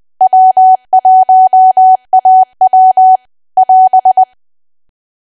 Morse code, heard as a keyed tone, about 20 wpm, 750 Hz. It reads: W1AW <AS>
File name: cwa / W1AW-Level3 / 20WPM